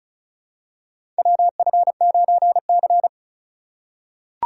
Morse code, heard as a keyed tone, about 34 wpm, 700 Hz. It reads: WF9C
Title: USA Call Signs - 4 Characters - 2x Character Spacing - 35wpm